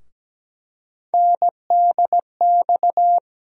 Morse code, heard as a keyed tone, about 17 wpm, 700 Hz. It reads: NDX
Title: USA Call Sign Postfixes - 3 Letters 17wpm